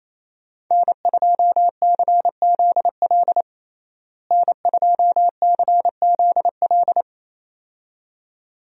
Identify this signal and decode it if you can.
Morse code, heard as a keyed tone, about 28 wpm, 700 Hz. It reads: N2CZL N2CZL
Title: USA Call Signs - 5 Characters - Repeated 2x - 28wpm